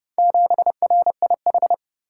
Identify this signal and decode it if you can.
Morse code, heard as a keyed tone, about 29 wpm, 700 Hz. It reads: 7RIH